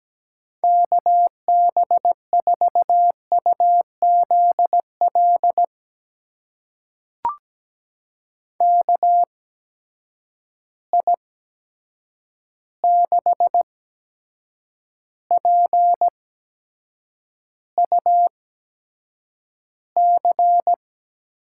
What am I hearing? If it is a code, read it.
Morse code, heard as a keyed tone, about 17 wpm, spaced as for 7 wpm, 700 Hz. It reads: KB4UZL KI6PUC
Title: USA Call Signs - 6 Characters - 8x Character Spacing - 17wpm